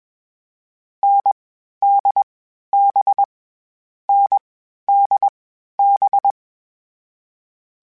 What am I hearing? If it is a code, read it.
Morse code, heard as a keyed tone, about 21 wpm, 800 Hz. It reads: N D B N D B
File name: Difficult Combos / Difficult Combos-mp3s / NDB NDB6 @20